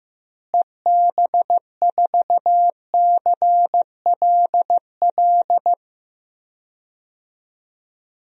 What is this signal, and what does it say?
Morse code, heard as a keyed tone, about 16 wpm, 700 Hz. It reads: EB4CLL
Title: DXCC-100 Call Signs - 6 Characters - Repeated 3x - 15wpm